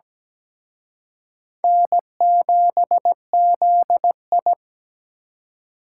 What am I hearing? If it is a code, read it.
Morse code, heard as a keyed tone, about 20 wpm, 700 Hz. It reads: N7ZI
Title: Speed Racing - USA 1x2 Call Signs 17wpm